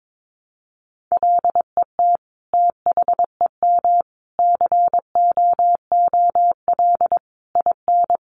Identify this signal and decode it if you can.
Morse code, heard as a keyed tone, about 22 wpm, 700 Hz. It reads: LET THEM COOL IN
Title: Sentences from Top 1200 Words 22wpm